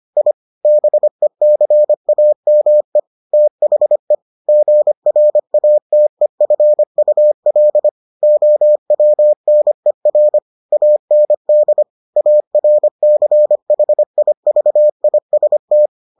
Morse code, heard as a keyed tone, about 25 wpm, 600 Hz. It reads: I BECAME THE GRATEFUL OWNER AND ARCHIVIST